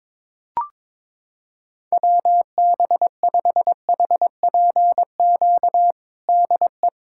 Morse code, heard as a keyed tone, about 22 wpm, 700 Hz. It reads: WB5HPQ DE
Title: USA Callsign DE Callsign - 6 Characters - 22wpm